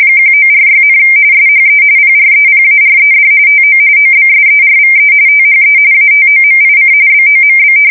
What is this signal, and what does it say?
RTTY, about 45.45 baud. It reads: 1946, SHE PREPARED TO OBSERVE THE ATOMIC BOMB
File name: fsk170 / battleshipnj